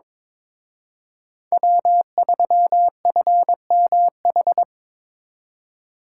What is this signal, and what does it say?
Morse code, heard as a keyed tone, about 26 wpm, 700 Hz. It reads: W3FMH
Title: Speed Racing - USA 1x3 Call Signs 22wpm